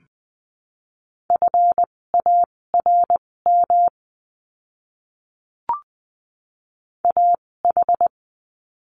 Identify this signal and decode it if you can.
Morse code, heard as a keyed tone, about 20 wpm, spaced as for 16 wpm, 700 Hz. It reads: FARM AH
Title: 256 Lesson 42a - Words 12-20wpm